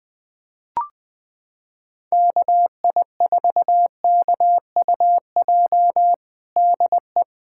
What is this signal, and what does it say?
Morse code, heard as a keyed tone, about 21 wpm, 700 Hz. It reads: KI4KUJ DE
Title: USA Callsign DE Callsign 20wpm